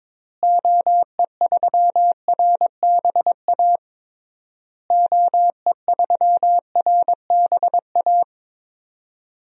Morse code, heard as a keyed tone, about 22 wpm, 700 Hz. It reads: OE3RBA OE3RBA
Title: DXCC-100 Call Signs - 6 Characters - Repeated 3x - 22wpm